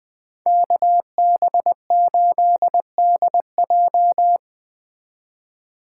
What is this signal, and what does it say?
Morse code, heard as a keyed tone, about 20 wpm, 700 Hz. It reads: KB8DJ
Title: Speed Racing - USA 2x2 Call Signs 17wpm